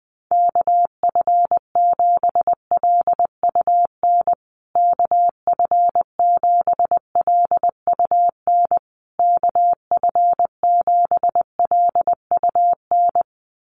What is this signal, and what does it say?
Morse code, heard as a keyed tone, about 20 wpm, 700 Hz. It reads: KF7LUN KF7LUN KF7LUN
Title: USA Call Signs Repeated 3x 20wpm